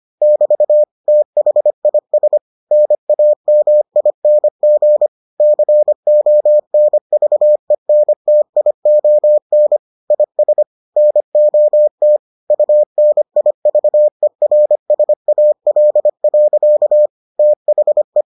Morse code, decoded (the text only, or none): <BT> THIS NAMING CONVENTION IS NOT UNIVERSAL. THE